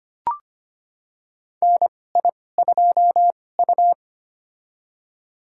Morse code, heard as a keyed tone, about 25 wpm, spaced as for 18 wpm, 700 Hz. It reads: NI2U
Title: USA Call Signs - 4 Characters - 2x Character Spacing - 25wpm